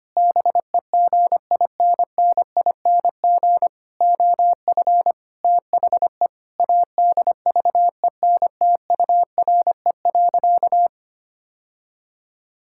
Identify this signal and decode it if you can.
Morse code, heard as a keyed tone, about 25 wpm, 700 Hz. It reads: BEGINNING OF THE ADVENTURE.
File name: practice / 25wpm / Sentences with 4-Word Phrases - 25wpm